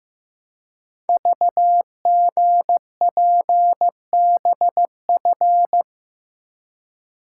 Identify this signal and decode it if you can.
Morse code, heard as a keyed tone, about 15 wpm, 690 Hz. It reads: VGPBF
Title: Code Groups - 5 Letters - 15wpm